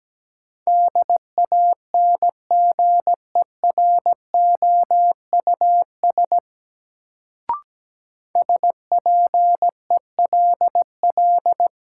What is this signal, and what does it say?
Morse code, heard as a keyed tone, about 17 wpm, 700 Hz. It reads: DANGEROUS SPELL
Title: Sentences from Top 1200 Words - New Word Drill Alt - 17wpm